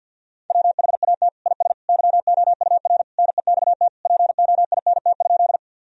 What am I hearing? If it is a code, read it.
Morse code, heard as a keyed tone, about 46 wpm, 700 Hz. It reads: WHAT IS YOUR NEXT POINT?
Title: Sentences from Top 1000 Words - Review Part 1 - 50wpm